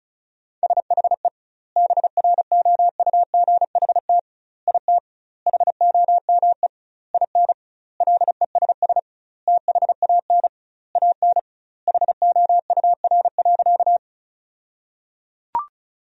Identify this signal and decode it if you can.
Morse code, heard as a keyed tone, about 34 wpm, 700 Hz. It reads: SHE BROUGHT IT HOME IN LESS THAN AN HOUR.
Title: 2x Word-Spacing - Sentences from Top 500 Words - 35wpm